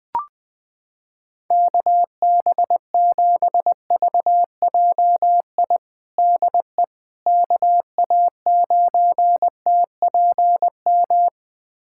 Morse code, heard as a keyed tone, about 19 wpm, 700 Hz. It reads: KB7VJI DE KA9TPM
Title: USA Callsign DE Callsign - 6 Characters - 20wpm